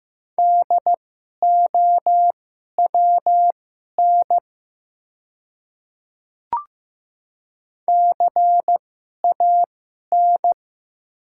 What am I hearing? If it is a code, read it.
Morse code, heard as a keyed tone, about 15 wpm, spaced as for 11 wpm, 700 Hz. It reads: DOWN CAN
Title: 2x Character Spacing - Top 200 Words - 15wpm